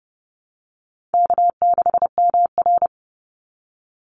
Morse code, heard as a keyed tone, about 30 wpm, 700 Hz. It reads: K6MR
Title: Easy Call Signs Repeated 2x - 30wpm